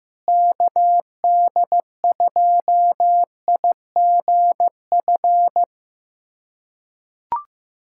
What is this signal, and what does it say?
Morse code, heard as a keyed tone, about 15 wpm, 700 Hz. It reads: KD2IGF
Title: USA Callsign DE Callsign - 6 Characters - 15wpm